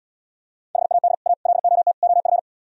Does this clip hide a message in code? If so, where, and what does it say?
Yes: Morse code from 0.8 to 2.4 s, heard as a keyed tone, about 79 wpm, 700 Hz. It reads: HAD A LOT OF